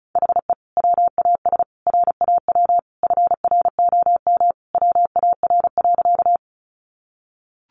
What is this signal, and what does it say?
Morse code, heard as a keyed tone, about 34 wpm, 700 Hz. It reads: HE WAS RAW FROM WAR.
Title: Sentences from Top 1500 Words 35wpm